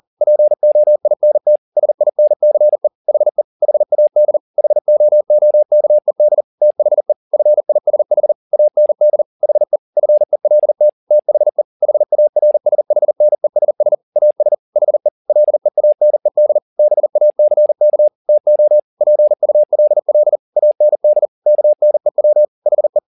Morse code, heard as a keyed tone, about 40 wpm, 600 Hz. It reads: POINT SINCE HE HAD HOOKED THE FISH AND HE FELT THE HARSHNESS AS HE LEANED BACK TO PULL AND KNEW HE